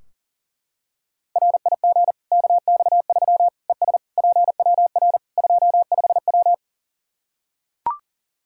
Morse code, heard as a keyed tone, about 40 wpm, 700 Hz. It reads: RIG KX3 ES PWR 25W
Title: K1USN SST QSO Practice 40wpm